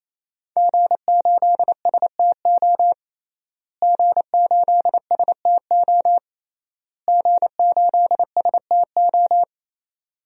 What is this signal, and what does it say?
Morse code, heard as a keyed tone, about 27 wpm, 700 Hz. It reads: G8STO G8STO G8STO
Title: DXCC-100 Call Signs - 5 Characters - Repeated 3x - 28wpm